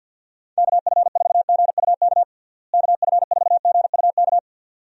Morse code, heard as a keyed tone, about 51 wpm, 700 Hz. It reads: KF4GUK KF4GUK
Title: USA Call Signs - 6 Characters - Repeated 2x - 50wpm